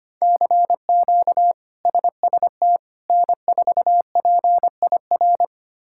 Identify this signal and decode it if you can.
Morse code, heard as a keyed tone, about 25 wpm, 700 Hz. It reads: CQ SST N4PIR
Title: K1USN SST QSO Practice - 25wpm